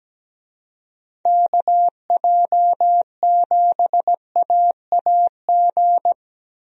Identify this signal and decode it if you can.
Morse code, heard as a keyed tone, about 17 wpm, 700 Hz. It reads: KJ7AAG
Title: USA Call Signs - 6 Characters - Repeated 2x - 17wpm